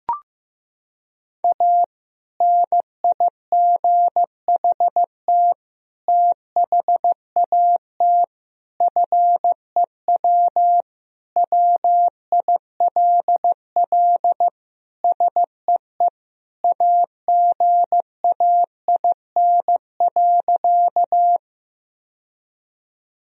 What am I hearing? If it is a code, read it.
Morse code, heard as a keyed tone, about 15 wpm, 700 Hz. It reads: A NIGHT THAT FEW WILL SEE AGAIN.
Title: Sentences from Top 1000 Words - Review Part 1 - 15wpm